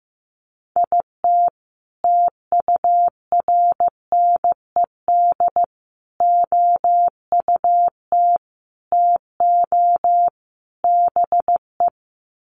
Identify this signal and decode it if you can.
Morse code, heard as a keyed tone, about 15 wpm, 700 Hz. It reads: IT TURNED OUT TO BE
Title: Sets of 5 Words 15wpm